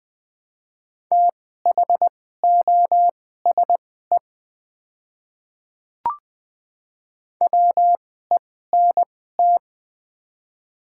Morse code, heard as a keyed tone, about 20 wpm, spaced as for 14 wpm, 700 Hz. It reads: THOSE WENT
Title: 2x Character Spacing - Top 500 Words - 20wpm